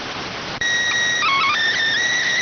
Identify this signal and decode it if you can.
SSTV (slow-scan television), PD 290.